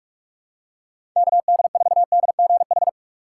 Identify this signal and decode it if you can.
Morse code, heard as a keyed tone, about 46 wpm, 700 Hz. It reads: KD4DGH
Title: USA Call Signs - 6 Characters - 4x Character Spacing - 45wpm